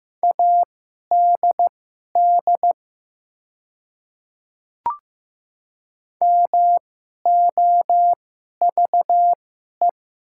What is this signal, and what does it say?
Morse code, heard as a keyed tone, about 15 wpm, spaced as for 11 wpm, 700 Hz. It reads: ADD MOVE
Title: 2x Character Spacing - Top 200 Words - 15wpm